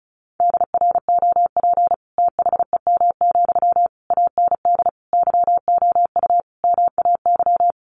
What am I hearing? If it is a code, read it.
Morse code, heard as a keyed tone, about 34 wpm, 700 Hz. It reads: DROP THEM, AND YOU MAY
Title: Sentences from Top 1400 Words 35wpm